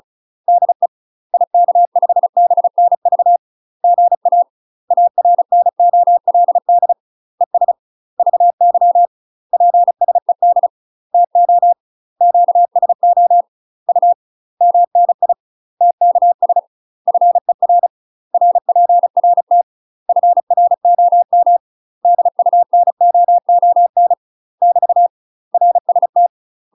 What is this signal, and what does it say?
Morse code, heard as a keyed tone, about 35 wpm, 700 Hz. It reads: DE IK5BNV GA ARNOLD ES VY PSED TO QSO U MNI TKS FER RPRT FROM DUNOON <BT> RST
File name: UK MASTER QSOs TRAINING / 35WPM / UK MASTER QSOs 0004